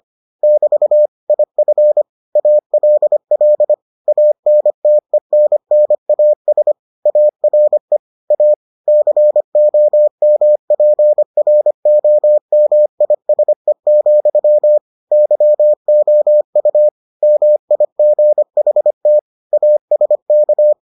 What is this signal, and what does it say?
Morse code, heard as a keyed tone, about 25 wpm, 600 Hz. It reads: <BT> IF ALL ANTENNAS ARE A COMPROMISE, YOU MIGHT ASK